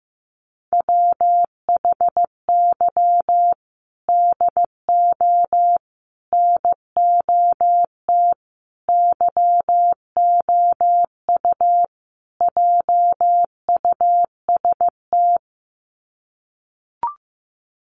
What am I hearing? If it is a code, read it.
Morse code, heard as a keyed tone, about 15 wpm, 700 Hz. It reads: WHY DO NOT YOU JUST
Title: Sets of 5 Words 15wpm